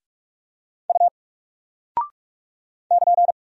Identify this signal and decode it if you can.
Morse code, heard as a keyed tone, about 44 wpm, 700 Hz. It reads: U <KN>
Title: Single Character (Letter:Number:Punctuation:Prosign) 45wpm